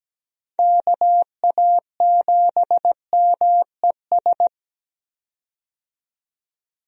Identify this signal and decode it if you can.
Morse code, heard as a keyed tone, about 17 wpm, 700 Hz. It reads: KA7MES
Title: USA Call Signs - 6 Characters - 17wpm